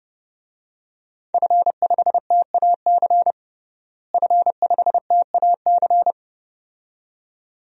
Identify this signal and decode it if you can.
Morse code, heard as a keyed tone, about 30 wpm, 700 Hz. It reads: F5TAC F5TAC
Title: DXCC-100 Call Signs - 5 Characters - Repeated 2x - 30wpm